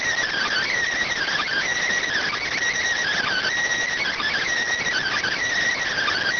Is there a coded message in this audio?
Yes: Emergency Alert System (SAME) tones (header bursts) from 1.1 to 5.4 s, heard as harsh buzzing bursts.